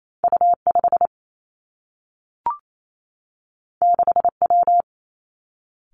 Morse code, heard as a keyed tone, about 28 wpm, 700 Hz. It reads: U 5 6 W
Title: Two Challenging Letter-Number Combinations 28wpm